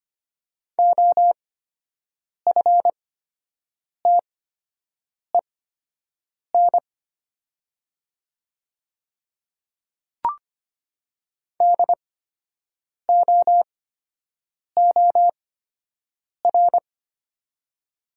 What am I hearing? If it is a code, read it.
Morse code, heard as a keyed tone, about 25 wpm, spaced as for 7 wpm, 700 Hz. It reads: OFTEN DOOR
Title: 8x Character Spacing - Top 300 Words - 25wpm